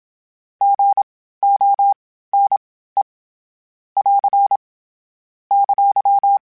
Morse code, heard as a keyed tone, about 27 wpm, spaced as for 15 wpm, 800 Hz. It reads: GONE <AR> !
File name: Misc / Misc-mp3s / Gotten @25